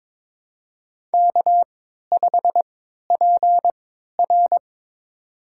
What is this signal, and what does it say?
Morse code, heard as a keyed tone, about 22 wpm, spaced as for 13 wpm, 700 Hz. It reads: K5PR
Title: USA Call Signs - 4 Characters - 3x Character Spacing - 22wpm